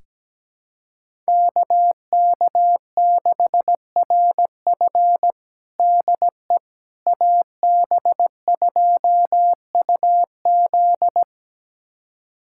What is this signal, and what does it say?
Morse code, heard as a keyed tone, about 17 wpm, 700 Hz. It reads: KK6RF DE AB2UZ
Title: USA Callsign DE Callsign - 5 Characters - 17wpm